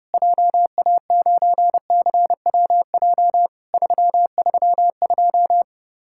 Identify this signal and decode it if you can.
Morse code, heard as a keyed tone, about 30 wpm, 700 Hz. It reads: JA9CWJ 332
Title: Worked All Europe DX Contest - QTC Exchange - 30wpm